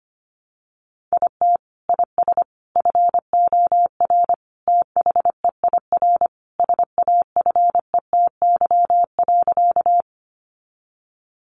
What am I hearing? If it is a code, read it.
Morse code, heard as a keyed tone, about 25 wpm, 700 Hz. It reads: IT IS FOR THEIR SAFETY.
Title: Sentences from Top 700 Words 25wpm